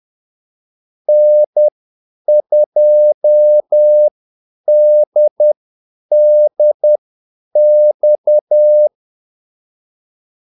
Morse code, heard as a keyed tone, about 10 wpm, spaced as for 8 wpm, 600 Hz. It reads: N2DDX
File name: Callsigns-10-8wpm